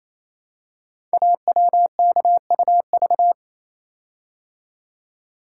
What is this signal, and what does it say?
Morse code, heard as a keyed tone, about 29 wpm, 700 Hz. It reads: AWKUV
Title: Code Groups - 5 Letters - 28wpm